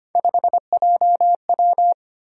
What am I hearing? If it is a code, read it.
Morse code, heard as a keyed tone, about 25 wpm, 700 Hz. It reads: 5JW
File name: Three Letter-Number Combinations - 25wpm